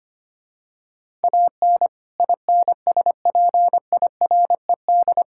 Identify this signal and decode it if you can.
Morse code, heard as a keyed tone, about 25 wpm, 700 Hz. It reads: AN INSPIRED
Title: Pangram Challenge v2 - 25wpm